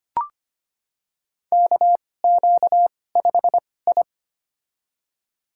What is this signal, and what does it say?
Morse code, heard as a keyed tone, about 25 wpm, spaced as for 18 wpm, 700 Hz. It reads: KQ5I
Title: USA Call Signs - 4 Characters - 2x Character Spacing - 25wpm